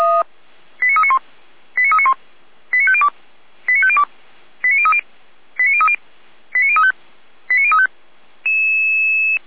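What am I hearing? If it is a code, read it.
Five-tone selective call (ZVEI), heard as quick runs of five tones, tones of about 80 ms. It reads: 89391 89391 89502 89502 89030 89030 89035 89035
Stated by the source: ZVEI five-tone selective call 89050, received on an SDR.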